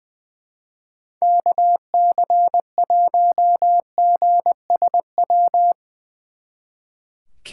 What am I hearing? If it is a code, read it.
Morse code, heard as a keyed tone, about 23 wpm, 700 Hz. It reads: KC1GSW
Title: Speed Racing - USA Call Signs 20wpm